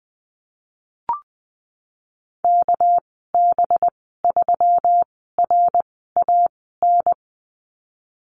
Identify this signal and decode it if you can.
Morse code, heard as a keyed tone, about 20 wpm, spaced as for 15 wpm, 700 Hz. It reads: KB3RAN
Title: 200 Lesson 32g - Letter k - Call Signs 20wpm